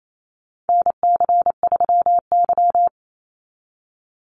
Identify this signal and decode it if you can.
Morse code, heard as a keyed tone, about 28 wpm, 700 Hz. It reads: NC3Y
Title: Call Signs Repeated 3x - 28wpm